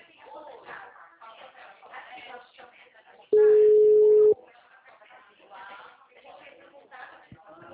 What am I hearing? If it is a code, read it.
Ringback tone (Europe, 425 Hz), with speech.